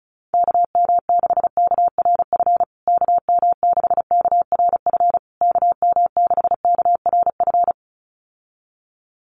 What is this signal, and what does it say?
Morse code, heard as a keyed tone, about 34 wpm, 700 Hz. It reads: KM6KRF KM6KRF KM6KRF
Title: USA Call Signs Repeated 3x 35wpm